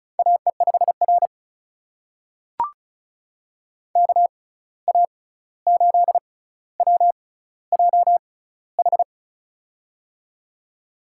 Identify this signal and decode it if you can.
Morse code, heard as a keyed tone, about 36 wpm, spaced as for 16 wpm, 700 Hz. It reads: AE5R KA8WJH
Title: USA Call Signs - 6x Character Spacing - 35wpm